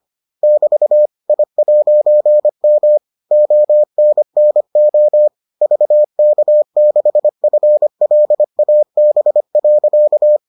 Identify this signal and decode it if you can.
Morse code, heard as a keyed tone, about 25 wpm, 600 Hz. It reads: <BT> I'M ONNO VK6FLAB.